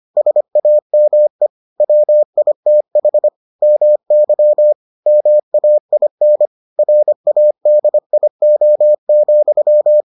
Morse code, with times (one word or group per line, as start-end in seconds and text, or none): 0.2-1.5 SAME
1.8-3.3 WITH
3.6-4.7 MY
5.1-6.5 MAIN
6.8-10.0 RADIO,